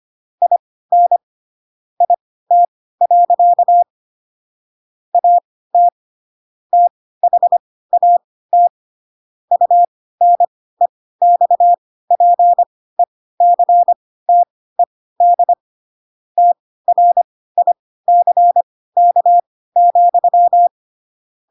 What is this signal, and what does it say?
Morse code, heard as a keyed tone, about 25 wpm, spaced as for 16 wpm, 700 Hz. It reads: IN IT. AT THAT UNEXPECTED TRICK,
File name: Pinocchio / Pinocchio ch3 10 25 ik1hge